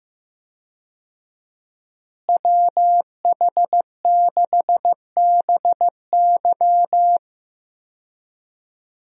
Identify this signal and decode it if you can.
Morse code, heard as a keyed tone, about 15 wpm, 700 Hz. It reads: WH6BY
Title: USA Call Signs - 5 Characters - Repeated 2x - 15wpm